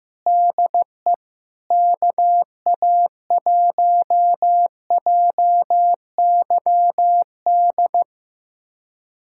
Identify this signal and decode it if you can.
Morse code, heard as a keyed tone, about 15 wpm, 700 Hz. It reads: DE KA1JYD
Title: USA Callsign DE Callsign 15wpm